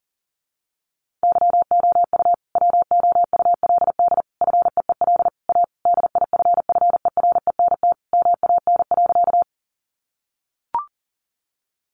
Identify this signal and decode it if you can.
Morse code, heard as a keyed tone, about 40 wpm, 700 Hz. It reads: YOU WOULD FEEL A DIFFERENT MAN.
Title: Sentences from Top 600 Words 40wpm